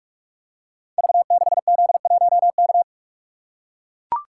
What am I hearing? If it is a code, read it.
Morse code, heard as a keyed tone, about 44 wpm, 700 Hz. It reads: V6Z1K
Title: Code Groups - 5 Letter:Numbers - 45wpm